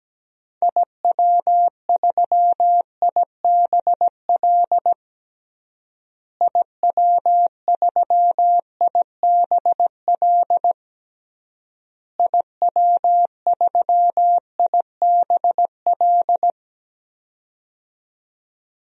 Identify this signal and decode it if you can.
Morse code, heard as a keyed tone, about 17 wpm, 700 Hz. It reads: IW3IBL IW3IBL IW3IBL
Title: DXCC-100 Call Signs - 6 Characters - Repeated 3x - 17wpm